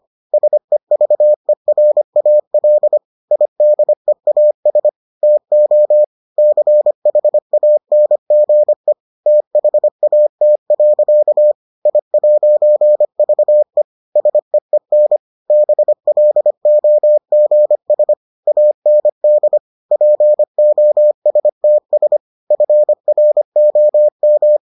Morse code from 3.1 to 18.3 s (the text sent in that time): IDEAS TO CHANGE THAT. I'VE SEEN BLOGS